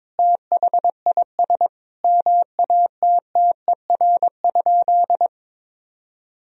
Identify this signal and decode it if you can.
Morse code, heard as a keyed tone, about 22 wpm, 700 Hz. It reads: THIS MATTER?
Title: Sentences from Top 1000 Words - Review Part 1 - 22wpm